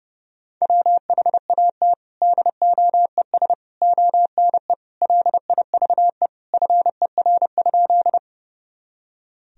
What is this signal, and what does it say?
Morse code, heard as a keyed tone, about 32 wpm, 700 Hz. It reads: WHAT DOES ONE LIVE FER?
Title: Sentences from Top 200 Words and Abbreviations 30wpm